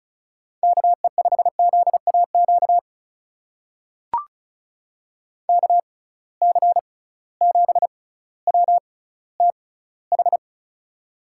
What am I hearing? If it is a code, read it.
Morse code, heard as a keyed tone, about 36 wpm, spaced as for 18 wpm, 700 Hz. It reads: KE5ZAQ KC7WTH
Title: USA Call Signs - 6 Characters - 6x Character Spacing - 35wpm